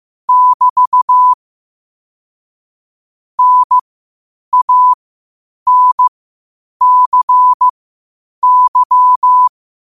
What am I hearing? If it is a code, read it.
Morse code, heard as a keyed tone, about 15 wpm, spaced as for 8 wpm, 1000 Hz. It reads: <BT> NANCY
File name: qotdfiles / 8wpm